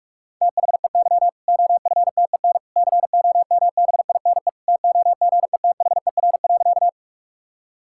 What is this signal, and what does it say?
Morse code, heard as a keyed tone, about 46 wpm, 700 Hz. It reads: THEY OFTEN COMBINE TOGETHER.
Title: Sentences from Top 1600 Words 45wpm